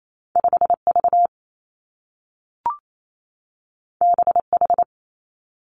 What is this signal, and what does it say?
Morse code, heard as a keyed tone, about 28 wpm, 700 Hz. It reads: 5 V B H
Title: Two Challenging Letter-Number Combinations 28wpm